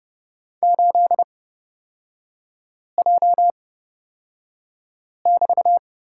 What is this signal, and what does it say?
Morse code, heard as a keyed tone, about 30 wpm, spaced as for 5 wpm, 700 Hz. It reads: 8J<BT>
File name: Lesson 36e - BT - Three Characters 30wpm